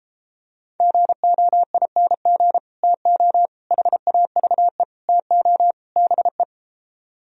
Morse code, heard as a keyed tone, about 33 wpm, 700 Hz. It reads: GOING TO HAVE TO BE